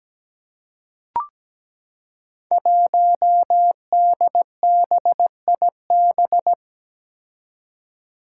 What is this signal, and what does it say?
Morse code, heard as a keyed tone, about 17 wpm, 700 Hz. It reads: 1DBIB
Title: Five Challenging Letter-Number Combinations 17wpm